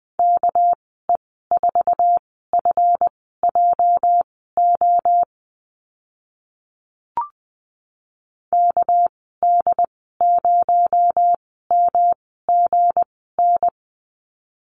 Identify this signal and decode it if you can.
Morse code, heard as a keyed tone, about 20 wpm, spaced as for 14 wpm, 700 Hz. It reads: KE4FJO KD0MGN
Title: 245 Lesson 39g - Letter z - Call Signs 20wpm